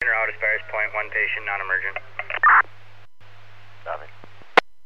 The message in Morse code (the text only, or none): none